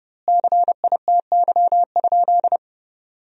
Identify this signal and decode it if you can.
Morse code, heard as a keyed tone, about 32 wpm, 700 Hz. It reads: CITY?
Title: 2x Word-Spacing - Sentences from Top 300 Words - 30wpm